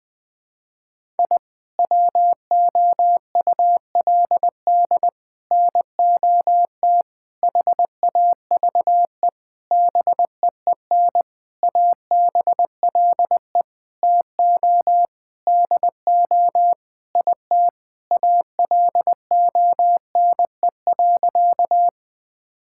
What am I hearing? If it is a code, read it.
Morse code, heard as a keyed tone, about 21 wpm, 700 Hz. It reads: I WOULD NOT HAVE BEEN ABLE TO DO IT ALONE.
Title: Sentences with 6-Word Phrases - Spoken Beginning - 20wpm